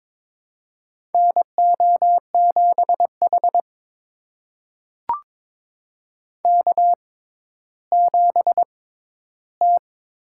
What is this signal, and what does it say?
Morse code, heard as a keyed tone, about 22 wpm, spaced as for 12 wpm, 700 Hz. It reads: NO7H K7T
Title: USA Call Signs - 6x Character Spacing - 22wpm